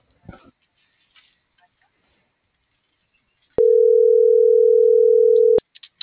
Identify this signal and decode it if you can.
Ringback tone (North America), 440 + 480 Hz.